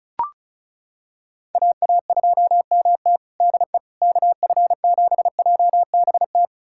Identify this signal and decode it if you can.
Morse code, heard as a keyed tone, about 36 wpm, 700 Hz. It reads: AA2MT DE KF7JBT
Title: USA Callsign DE Callsign 35wpm